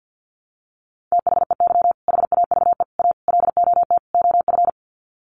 Morse code, heard as a keyed tone, about 50 wpm, 700 Hz. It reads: THEY HAVE A LOT OF
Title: Sets of 5 Words Encore2 50wpm